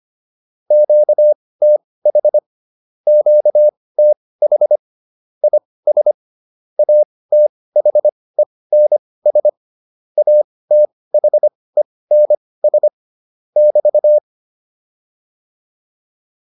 Morse code, heard as a keyed tone, about 25 wpm, spaced as for 18 wpm, 600 Hz. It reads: QTH QTH IS ATHENS ATHENS <BT>
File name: QSO408 18